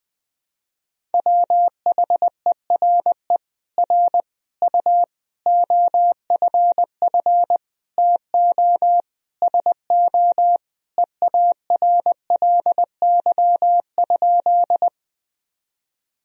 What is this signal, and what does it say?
Morse code, heard as a keyed tone, about 20 wpm, 700 Hz. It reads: WHERE R U OFF TO SO EARLY?
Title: Sentences from Top 400 Words and Abbreviations 20wpm